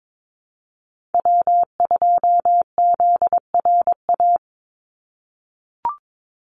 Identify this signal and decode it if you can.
Morse code, heard as a keyed tone, about 22 wpm, 700 Hz. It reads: W2ZRA
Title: Call Signs - 22wpm